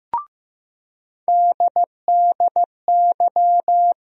Morse code, heard as a keyed tone, about 15 wpm, 700 Hz. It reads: DDY
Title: USA Call Sign Postfixes - 3 Letters 15wpm